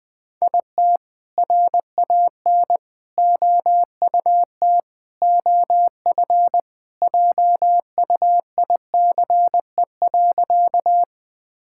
Morse code, heard as a keyed tone, about 21 wpm, 700 Hz. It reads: IT RAN OUT OF JUICE.
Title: All-Age Appropriate Jokes - 20wpm